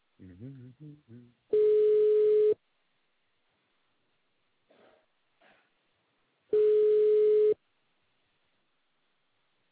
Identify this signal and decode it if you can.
Ringback tone (Europe, 425 Hz), with speech.